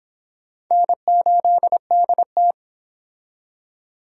Morse code, heard as a keyed tone, about 25 wpm, 700 Hz. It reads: N8DT